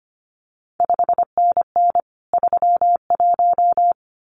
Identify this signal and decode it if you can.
Morse code, heard as a keyed tone, about 25 wpm, 700 Hz. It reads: 5NN 31
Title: CQ WW DX Contest Exchange - 25wpm